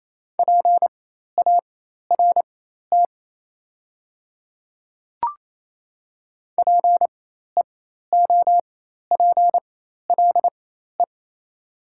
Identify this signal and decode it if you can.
Morse code, heard as a keyed tone, about 28 wpm, spaced as for 13 wpm, 700 Hz. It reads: PART PEOPLE
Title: 4x Character Spacing - Top 300 Words - 28wpm